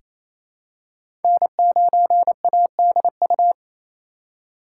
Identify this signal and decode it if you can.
Morse code, heard as a keyed tone, about 28 wpm, 700 Hz. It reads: N9ADU
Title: USA Call Signs - 8x Character Spacing - 28wpm